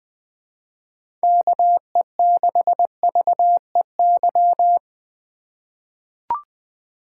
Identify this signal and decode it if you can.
Morse code, heard as a keyed tone, about 20 wpm, 700 Hz. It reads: KE6VEY
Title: USA Call Signs - 2x Character Spacing - 20wpm